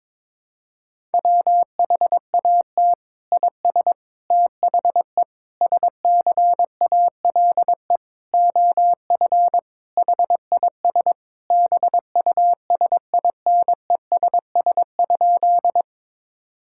Morse code, heard as a keyed tone, about 22 wpm, 700 Hz. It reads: WHAT IS THE SCALE OF HIS BUSINESS?
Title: Sentences from Top 1000 Words - Review Part 5 - 22wpm